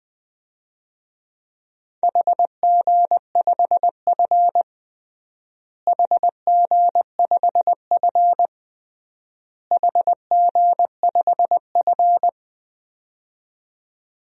Morse code, heard as a keyed tone, about 20 wpm, 700 Hz. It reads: HG5F HG5F HG5F
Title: DXCC-100 Call Signs - 4 Characters - Repeated 3x - 20wpm